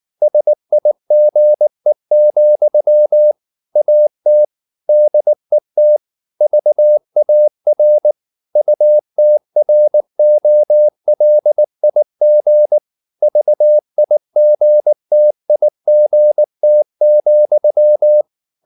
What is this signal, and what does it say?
Morse code, heard as a keyed tone, about 19 wpm, 600 Hz. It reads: SIGE, AT DET VAR UTROLIG VIGTIGT,
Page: Ferie 19 wpm